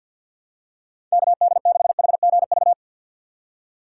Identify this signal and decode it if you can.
Morse code, heard as a keyed tone, about 50 wpm, 700 Hz. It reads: KD6HGV
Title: USA Call Signs - 6x Character Spacing - 50wpm